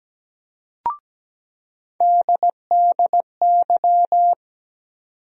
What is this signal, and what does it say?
Morse code, heard as a keyed tone, about 17 wpm, 700 Hz. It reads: DDY
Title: USA Call Sign Postfixes - 3 Letters 17wpm